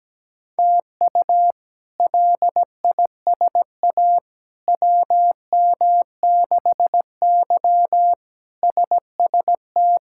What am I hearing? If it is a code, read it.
Morse code, heard as a keyed tone, about 17 wpm, 700 Hz. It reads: TU LISA WM6Y SST
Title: K1USN SST QSO Practice - 17wpm